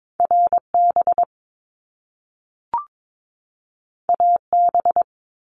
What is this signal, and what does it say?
Morse code, heard as a keyed tone, about 22 wpm, 700 Hz. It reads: RB AB
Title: Top Two Letter Combinations in Words 22wpm